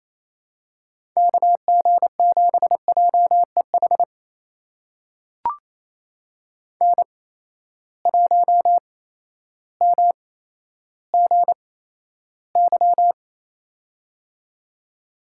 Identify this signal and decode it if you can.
Morse code, heard as a keyed tone, about 28 wpm, spaced as for 13 wpm, 700 Hz. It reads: KG7JEH N1MGY
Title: USA Call Signs - 8x Character Spacing - 28wpm